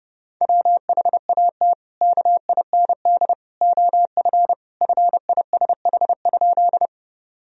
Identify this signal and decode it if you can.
Morse code, heard as a keyed tone, about 30 wpm, 700 Hz. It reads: WHAT KIND OF FISH?
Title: Sentences from Top 1000 Words - Review Part 1 - 30wpm